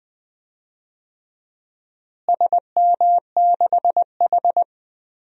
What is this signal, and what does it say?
Morse code, heard as a keyed tone, about 20 wpm, 700 Hz. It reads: SM6H
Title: DXCC-100 Call Signs - 4 Characters - Repeated 2x - 20wpm